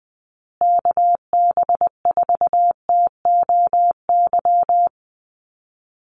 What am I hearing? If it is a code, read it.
Morse code, heard as a keyed tone, about 20 wpm, 700 Hz. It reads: KB4TOY
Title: USA Call Signs Repeated 2x 20wpm